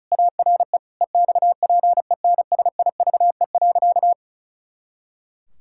Morse code, heard as a keyed tone, about 36 wpm, 700 Hz. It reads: ARE EXPENSIVE.
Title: Pangram Challenge v2 - 35wpm